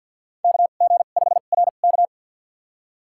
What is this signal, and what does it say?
Morse code, heard as a keyed tone, about 49 wpm, spaced as for 35 wpm, 700 Hz. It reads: KG5RK
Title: USA Call Signs - 2x Character Spacing - 50wpm